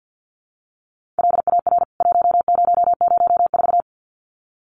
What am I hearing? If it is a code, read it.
Morse code, heard as a keyed tone, about 50 wpm, 700 Hz. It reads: LAR 1004
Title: CWT Exchange 50wpm